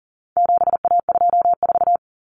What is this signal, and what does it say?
Morse code, heard as a keyed tone, about 40 wpm, 700 Hz. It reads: 7A24
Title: Common Cut Numbers - 4-Digit 40wpm